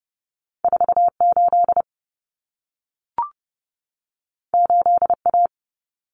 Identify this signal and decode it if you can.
Morse code, heard as a keyed tone, about 30 wpm, 700 Hz. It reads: 48 8A
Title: Common Cut Numbers - 2-Digit 30wpm